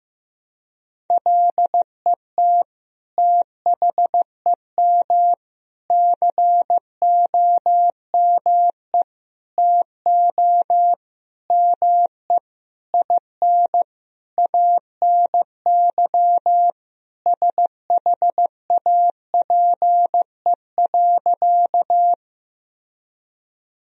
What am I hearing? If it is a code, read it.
Morse code, heard as a keyed tone, about 15 wpm, 700 Hz. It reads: LET THEM COME TO ME IN ANY SHAPE.
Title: Sentences from Top 1000 Words - Review Part 3 - 15wpm